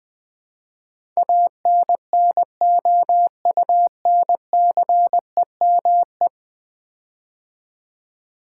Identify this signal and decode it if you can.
Morse code, heard as a keyed tone, about 20 wpm, 700 Hz. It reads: ANNOUNCEME
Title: Build A Word - 5 Letters At A Time - 12-Letter Words - 20wpm